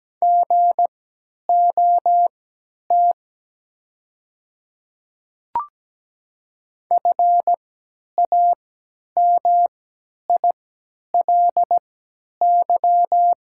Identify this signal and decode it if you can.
Morse code, heard as a keyed tone, about 17 wpm, spaced as for 10 wpm, 690 Hz. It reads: GOT FAMILY
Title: 3x Character Spacing - Top 500 Words - 17wpm